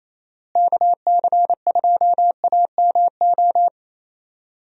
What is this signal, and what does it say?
Morse code, heard as a keyed tone, about 27 wpm, 700 Hz. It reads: KC2AMO
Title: USA Call Signs - 6 Characters - Repeated 2x - 28wpm